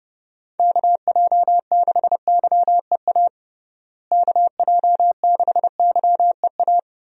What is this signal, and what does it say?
Morse code, heard as a keyed tone, about 30 wpm, 700 Hz. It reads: KJ6YEA KJ6YEA
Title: USA Call Signs - 6 Characters - Repeated 2x - 30wpm